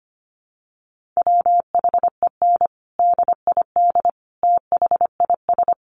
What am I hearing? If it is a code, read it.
Morse code, heard as a keyed tone, about 25 wpm, 700 Hz. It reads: WHEN DID THIS
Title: Sentences from Top 400 Words 25wpm